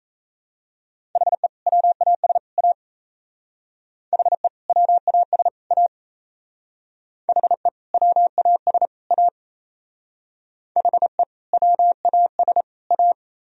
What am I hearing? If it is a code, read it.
Morse code, heard as a keyed tone, about 32 wpm, 700 Hz. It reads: HE WAS A HE WAS A HE WAS A HE WAS A
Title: Speed-Racing - Sets of 3 Words 28wpm